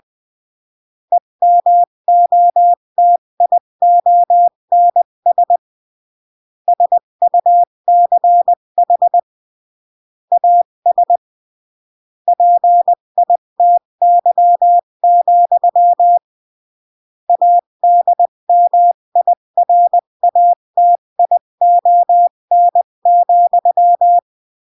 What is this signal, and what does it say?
Morse code, heard as a keyed tone, about 20 wpm, 700 Hz. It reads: EMOTIONS SUCH AS PITY, ADMIRATION,
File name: CW / jodorowsky / 82 commandments 20-15